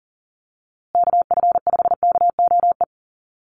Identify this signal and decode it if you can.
Morse code, heard as a keyed tone, about 39 wpm, 700 Hz. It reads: KF5KOE
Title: USA Call Signs Repeated 3x 40wpm